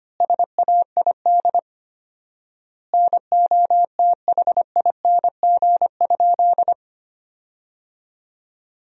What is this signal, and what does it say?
Morse code, heard as a keyed tone, about 25 wpm, 700 Hz. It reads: SAID NOTHING?
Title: 4x Word-Spacing - Sentences from Top 500 Words - 25wpm